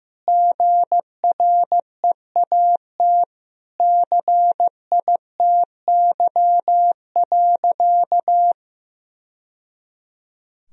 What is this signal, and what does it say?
Morse code, heard as a keyed tone, about 15 wpm, 700 Hz. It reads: GREAT CITY.
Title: Sentences from Top 1000 Words - Review Part 1 - 15wpm